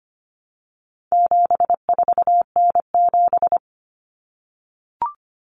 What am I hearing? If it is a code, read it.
Morse code, heard as a keyed tone, about 25 wpm, 700 Hz. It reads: 74N7
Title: Common Cut Numbers - 4-Digit 25wpm